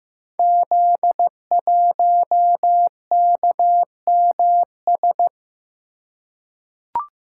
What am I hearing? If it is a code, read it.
Morse code, heard as a keyed tone, about 15 wpm, 700 Hz. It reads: Z1KMS
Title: Code Groups - 5 Letter:Numbers - 15wpm